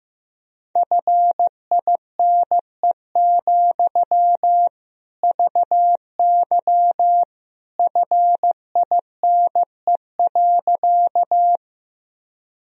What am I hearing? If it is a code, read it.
Morse code, heard as a keyed tone, about 15 wpm, 700 Hz. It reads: FINE, VY FINE.
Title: Sentences from Top 500 Words and Abbreviations 15wpm